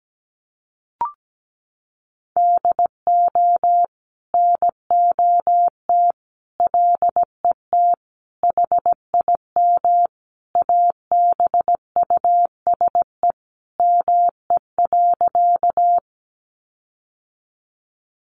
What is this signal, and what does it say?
Morse code, heard as a keyed tone, about 17 wpm, 700 Hz. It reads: DO NOT LET HIM ABUSE ME.
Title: Sentences from Top 1300 Words 17wpm